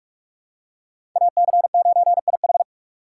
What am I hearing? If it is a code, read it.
Morse code, heard as a keyed tone, about 46 wpm, 700 Hz. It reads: AC9IH
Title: USA Call Signs - 2x Character Spacing - 45wpm